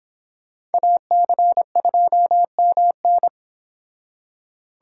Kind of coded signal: Morse code

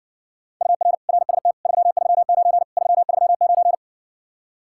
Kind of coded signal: Morse code

